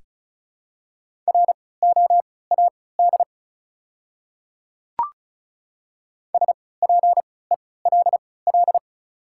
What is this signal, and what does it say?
Morse code, heard as a keyed tone, about 34 wpm, spaced as for 20 wpm, 700 Hz. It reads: ROAD SPELL
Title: 3x Character Spacing - Top 500 Words - 35wpm